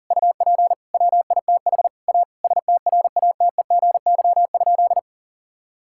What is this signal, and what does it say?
Morse code, heard as a keyed tone, about 40 wpm, 700 Hz. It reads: UP WITH A STRATEGY?
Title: Sentences from Top 1000 Words - Review Part 5 - 40wpm